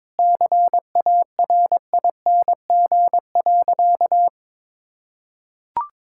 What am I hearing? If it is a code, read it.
Morse code, heard as a keyed tone, about 21 wpm, 700 Hz. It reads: CARING.
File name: practice / 22wpm / Sentences with 2-Word Phrases - 22wpm